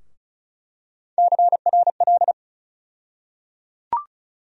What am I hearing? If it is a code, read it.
Morse code, heard as a keyed tone, about 36 wpm, 700 Hz. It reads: CRL
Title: USA Call Sign Postfixes - 3 Letters 35wpm